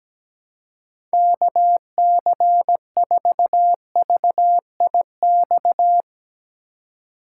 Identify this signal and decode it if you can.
Morse code, heard as a keyed tone, about 17 wpm, 700 Hz. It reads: KC4VIX
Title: USA Call Signs - 3x Character Spacing - 17wpm